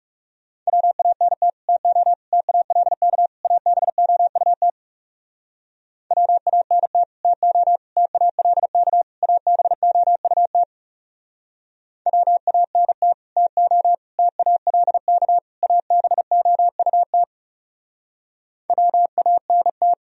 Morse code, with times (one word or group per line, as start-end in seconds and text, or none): 0.7-1.5 WANT
1.7-2.1 TO
2.3-3.3 TALK
3.4-4.7 ABOUT
6.1-7.0 WANT
7.2-7.8 TO
8.0-9.0 TALK
9.2-10.6 ABOUT
12.1-13.1 WANT
13.4-13.9 TO
14.2-15.4 TALK
15.6-17.2 ABOUT
18.7-19.9 WANT